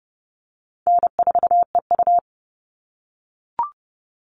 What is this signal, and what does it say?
Morse code, heard as a keyed tone, about 30 wpm, 700 Hz. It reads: N4EU
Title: All Cut Numbers - 4-Digit 30wpm